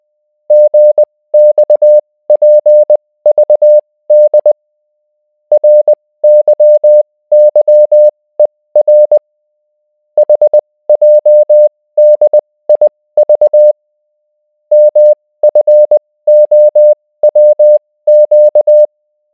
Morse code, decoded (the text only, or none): GXPVD RYYER HJDIV MFOWQ